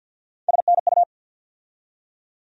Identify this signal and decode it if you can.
Morse code, heard as a keyed tone, about 52 wpm, 690 Hz. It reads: SNU
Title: USA Call Sign Postfixes - 3 Letters 50wpm